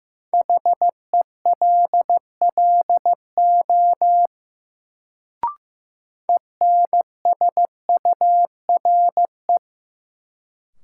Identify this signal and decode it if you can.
Morse code, heard as a keyed tone, about 15 wpm, 700 Hz. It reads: HELLO ENSURE
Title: Sentences from Top 1100 Words - New Word Drill Alt - 15wpm